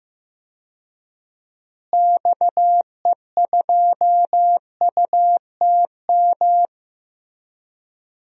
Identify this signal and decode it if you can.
Morse code, heard as a keyed tone, about 15 wpm, 700 Hz. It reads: XE2UTM
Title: DXCC-100 Call Signs - 6 Characters - Repeated 2x - 15wpm